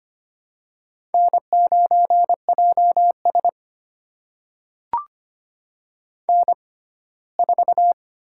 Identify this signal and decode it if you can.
Morse code, heard as a keyed tone, about 25 wpm, spaced as for 17 wpm, 700 Hz. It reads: N9JS N4
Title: USA Call Signs - 6x Character Spacing - 25wpm